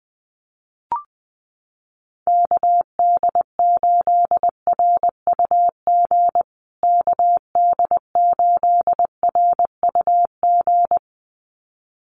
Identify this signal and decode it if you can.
Morse code, heard as a keyed tone, about 20 wpm, 700 Hz. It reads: KD8RUG KD8RUG
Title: USA Call Signs Repeated 2x 20wpm